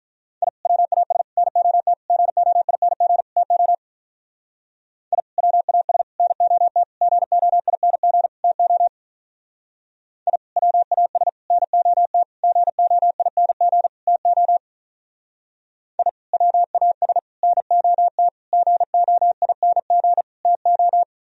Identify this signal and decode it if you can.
Morse code, heard as a keyed tone, about 42 wpm, 700 Hz. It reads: I WAS NOT GOING TO I WAS NOT GOING TO I WAS NOT GOING TO I WAS NOT GOING TO
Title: Speed-Racing - Sets of 5 Words 35wpm